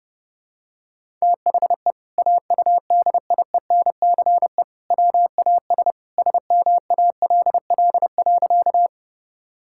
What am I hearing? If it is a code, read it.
Morse code, heard as a keyed tone, about 32 wpm, 700 Hz. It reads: THE AUDIENCE WAS SMALL.
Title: Sentences from Top 1000 Words - Review Part 5 - 30wpm